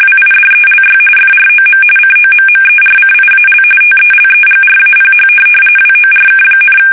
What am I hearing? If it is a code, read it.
RTTY, about 45.45 baud. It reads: WIDELY FROM COMMERCIAL TELEGRAPH WORK,
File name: fsk850